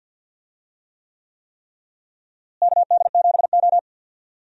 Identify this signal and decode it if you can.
Morse code, heard as a keyed tone, about 53 wpm, 700 Hz. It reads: KD7O
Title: USA Call Signs - 4 Characters - Repeated 3x - 50wpm